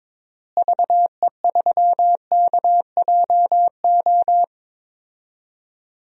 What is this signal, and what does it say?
Morse code, heard as a keyed tone, about 22 wpm, 700 Hz. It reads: VE3KJO
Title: DXCC-100 Call Signs - 6 Characters - Repeated 2x - 22wpm